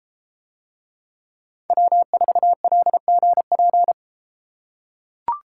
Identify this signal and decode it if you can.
Morse code, heard as a keyed tone, about 32 wpm, 700 Hz. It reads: W4LGP